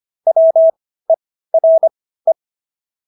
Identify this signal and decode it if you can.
Morse code, heard as a keyed tone, about 25 wpm, spaced as for 15 wpm, 650 Hz. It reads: WERE